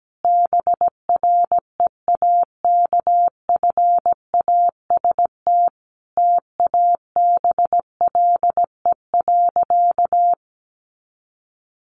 Morse code, heard as a keyed tone, about 17 wpm, 700 Hz. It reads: BREAKFAST TABLE.
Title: Sentences from Top 1500 Words 17wpm